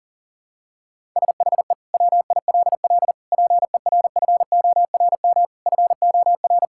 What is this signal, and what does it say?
Morse code, heard as a keyed tone, about 40 wpm, 700 Hz. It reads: SHE WILL PERFORM FOR
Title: Sentences from Top 1000 Words - Review Part 5 - 40wpm